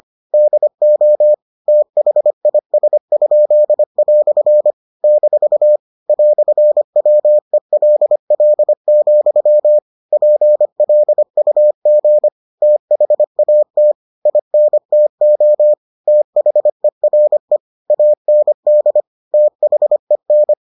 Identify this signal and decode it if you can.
Morse code, heard as a keyed tone, about 25 wpm, 600 Hz. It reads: DO THIS?" - "WELL, PLUG THAT INTO THERE AND THEN